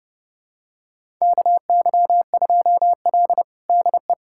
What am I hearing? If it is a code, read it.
Morse code, heard as a keyed tone, about 30 wpm, 700 Hz. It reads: KY2L DE
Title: USA Callsign DE Callsign 30wpm